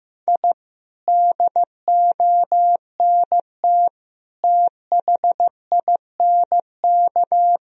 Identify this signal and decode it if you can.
Morse code, heard as a keyed tone, about 15 wpm, 700 Hz. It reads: I DONT THINK
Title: Sentences from Top 1000 Words - Review Part 4 - 15wpm